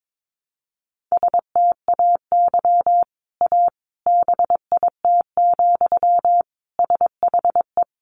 Morse code, heard as a keyed tone, about 22 wpm, 700 Hz. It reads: STAY A BIT, SHE
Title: Sentences from Top 900 Words 22wpm